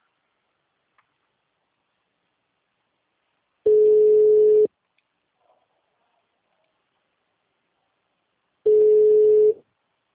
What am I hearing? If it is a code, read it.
Ringback tone (Europe, 425 Hz).